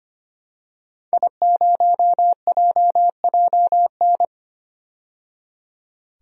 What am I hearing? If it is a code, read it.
Morse code, heard as a keyed tone, about 25 wpm, 700 Hz. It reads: I0JJN
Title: DXCC-100 Call Signs - 5 Characters - 25wpm